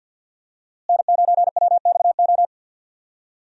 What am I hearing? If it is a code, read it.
Morse code, heard as a keyed tone, about 51 wpm, 700 Hz. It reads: N9WXO
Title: USA Call Signs - 5 Characters - Repeated 3x - 50wpm